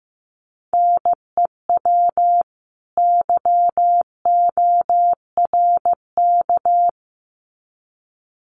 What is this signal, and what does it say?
Morse code, heard as a keyed tone, about 15 wpm, 700 Hz. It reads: NEW YORK
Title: Sets of 2 Words 15wpm